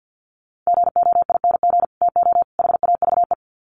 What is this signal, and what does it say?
Morse code, heard as a keyed tone, about 50 wpm, 700 Hz. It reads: GOING TO HAVE
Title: Sets of 3 Words 50wpm